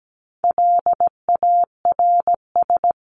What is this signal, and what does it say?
Morse code, heard as a keyed tone, about 17 wpm, 700 Hz. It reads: LARS
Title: CWT Exchange 17wpm